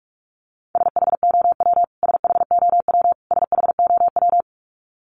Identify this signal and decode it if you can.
Morse code, heard as a keyed tone, about 44 wpm, 700 Hz. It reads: SHOW SHOW SHOW
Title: Top 200 Words Repeated 45wpm